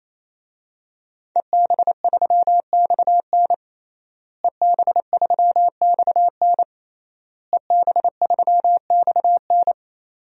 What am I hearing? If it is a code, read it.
Morse code, heard as a keyed tone, about 28 wpm, 700 Hz. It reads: EB3XN EB3XN EB3XN
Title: DXCC-100 Call Signs - 5 Characters - Repeated 3x - 28wpm